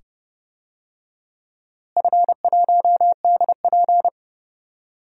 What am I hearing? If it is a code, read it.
Morse code, heard as a keyed tone, about 31 wpm, 700 Hz. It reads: F1DP
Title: DXCC-100 Call Signs - 4 Characters - Repeated 2x - 30wpm